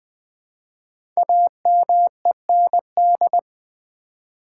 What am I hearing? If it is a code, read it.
Morse code, heard as a keyed tone, about 20 wpm, 700 Hz. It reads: AMEND
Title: 5 Letter Words - First 3 Letters Given - 20wpm